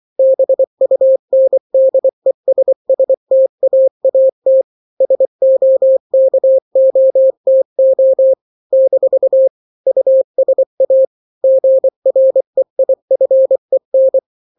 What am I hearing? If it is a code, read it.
Morse code, heard as a keyed tone, about 24 wpm, 530 Hz. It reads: BUNDESSTAAT SOKOTO - USA GREIFEN
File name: daily / 23wpm